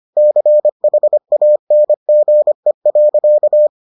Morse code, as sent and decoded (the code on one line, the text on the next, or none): -.-. .... .- -. --. .-.-.-
CHANGE.